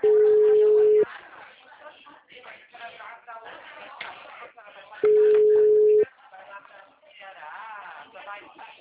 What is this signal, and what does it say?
Ringback tone (Europe, 425 Hz), with speech.
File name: out- -2023